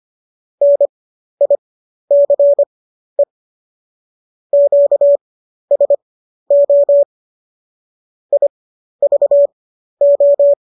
Morse code, heard as a keyed tone, about 25 wpm, spaced as for 12 wpm, 570 Hz. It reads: NICE QSO IVO
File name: qso012 12